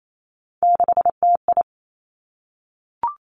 Morse code, heard as a keyed tone, about 28 wpm, 700 Hz. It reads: 6TI
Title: Three Challenging Letter-Number Combinations 28wpm